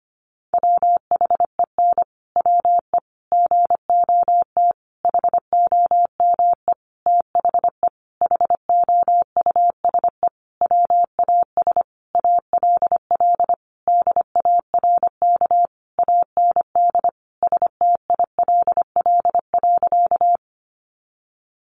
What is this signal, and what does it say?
Morse code, heard as a keyed tone, about 25 wpm, 700 Hz. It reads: WHEN WE GOT HOME THE HOUSE WAS ALL DARK AND STILL.
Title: Sentences from Top 500 Words 25wpm